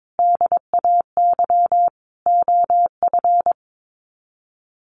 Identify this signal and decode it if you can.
Morse code, heard as a keyed tone, about 22 wpm, 700 Hz. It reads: DAY OF
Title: Sets of 5 Words Encore 22wpm